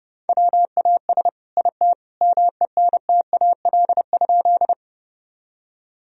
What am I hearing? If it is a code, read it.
Morse code, heard as a keyed tone, about 32 wpm, 700 Hz. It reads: WAS IT MENTAL?
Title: Sentences from Top 1000 Words - Review Part 4 - 30wpm